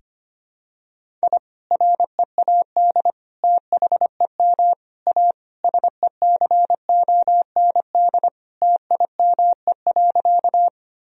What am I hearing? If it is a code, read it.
Morse code, heard as a keyed tone, about 25 wpm, 700 Hz. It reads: I READ THEM A SECOND TIME.
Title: Sentences from Top 1000 Words - Review Part 1 - 25wpm